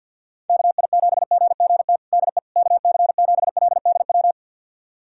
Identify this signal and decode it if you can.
Morse code, heard as a keyed tone, about 49 wpm, 700 Hz. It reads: KI7GGT DE KC7LDW
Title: USA Callsign DE Callsign - 6 Characters - 50wpm